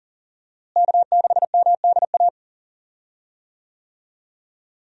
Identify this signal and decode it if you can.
Morse code, heard as a keyed tone, about 40 wpm, 700 Hz. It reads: K6MDA
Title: USA Call Signs - 5 Characters - Repeated 3x - 40wpm